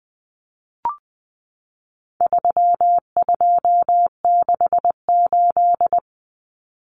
Morse code, heard as a keyed tone, about 20 wpm, 700 Hz. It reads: 3268
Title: Numbers - 4-Digit 20wpm